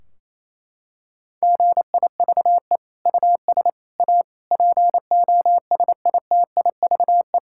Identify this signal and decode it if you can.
Morse code, heard as a keyed tone, about 27 wpm, 700 Hz. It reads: GIVE US A POSITIVE
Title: Sentences from Top 1600 Words 28wpm